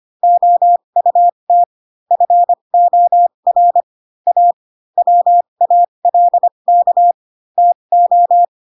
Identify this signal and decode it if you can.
Morse code, heard as a keyed tone, about 25 wpm, spaced as for 22 wpm, 700 Hz. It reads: OUT FOR A WALK TO